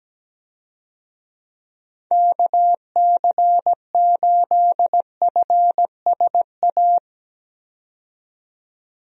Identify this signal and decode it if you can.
Morse code, heard as a keyed tone, about 17 wpm, 700 Hz. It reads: KC8FSA
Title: USA Call Signs - 6 Characters - 17wpm